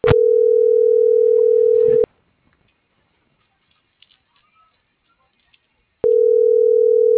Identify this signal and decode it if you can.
Ringback tone (North America), 440 + 480 Hz.